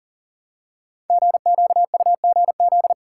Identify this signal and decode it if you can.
Morse code, heard as a keyed tone, about 41 wpm, 700 Hz. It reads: GQUGZ
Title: Code Groups - 5 Letters - 40wpm